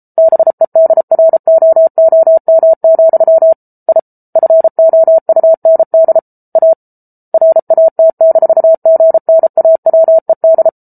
Morse code, heard as a keyed tone, about 34 wpm, 650 Hz. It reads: BEDROOM, I FOUND A RAT-GNAWED